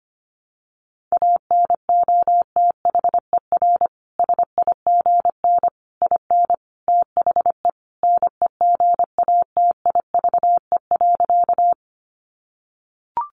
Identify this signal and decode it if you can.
Morse code, heard as a keyed tone, about 25 wpm, 700 Hz. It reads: ANOTHER SIGN IN THE NEGATIVE.
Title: Sentences from Top 1000 Words 25wpm